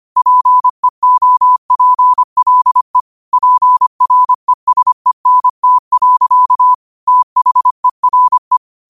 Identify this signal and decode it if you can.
Morse code, heard as a keyed tone, about 25 wpm, 1000 Hz. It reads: PEOPLE PRESENT. THERE